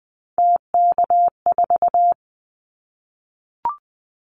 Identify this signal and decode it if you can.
Morse code, heard as a keyed tone, about 20 wpm, 700 Hz. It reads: TK4
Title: Callsign Prefixes for Top 100 DXCC - 20wpm